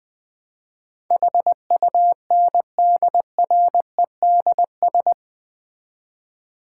Morse code, heard as a keyed tone, about 21 wpm, 700 Hz. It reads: HUNDREDS
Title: 4x Word-Spacing - Sets of 4 Words - 20wpm